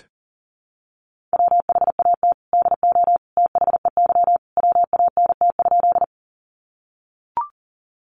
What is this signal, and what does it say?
Morse code, heard as a keyed tone, about 40 wpm, 700 Hz. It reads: WHAT DO THEY WANT?
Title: Sentences from Top 200 Words 40wpm